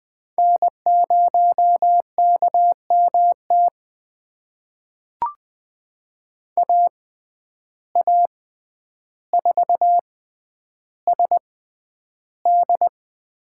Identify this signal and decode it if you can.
Morse code, heard as a keyed tone, about 20 wpm, spaced as for 10 wpm, 700 Hz. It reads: N0KMT AA4SD
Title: USA Call Signs - 5 Characters - 6x Character Spacing - 20wpm